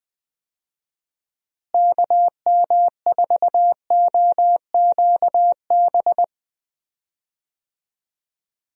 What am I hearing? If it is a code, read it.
Morse code, heard as a keyed tone, about 20 wpm, 700 Hz. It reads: KM4OQB
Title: USA Call Signs - 6 Characters - Repeated 2x - 20wpm